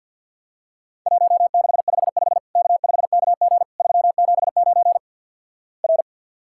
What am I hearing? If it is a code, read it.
Morse code, heard as a keyed tone, about 50 wpm, 700 Hz. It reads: 1655 K5KG 379 R
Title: Worked All Europe DX Contest - QTC Exchange with Repeats - 50wpm